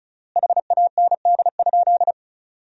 Morse code, heard as a keyed tone, about 34 wpm, 700 Hz. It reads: HAND?
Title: 3x Word-Spacing - Sentences from Top 200 Words - 35wpm